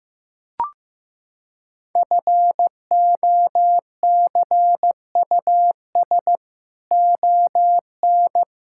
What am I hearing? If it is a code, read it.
Morse code, heard as a keyed tone, about 15 wpm, 700 Hz. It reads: FOCUS ON
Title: Sentences from Top 1000 Words - Review Part 5 - 15wpm